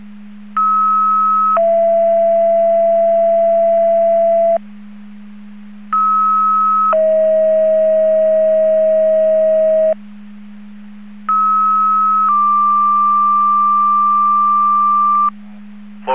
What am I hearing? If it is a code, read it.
Two-tone page, heard as a one-second tone, then a three-second tone, about 1290 then 670 Hz, about 1290 then 640 Hz, about 1290 then 1180 Hz. It is heard from 0.6 to 15.3 s.